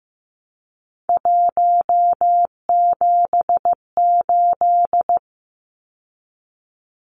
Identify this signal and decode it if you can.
Morse code, heard as a keyed tone, about 15 wpm, 700 Hz. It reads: 178
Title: Numbers - 3-Digit 15wpm